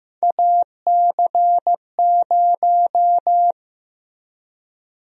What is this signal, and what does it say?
Morse code, heard as a keyed tone, about 15 wpm, 700 Hz. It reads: AC0
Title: USA Call Sign Prefixes - Repeated 2x - 15wpm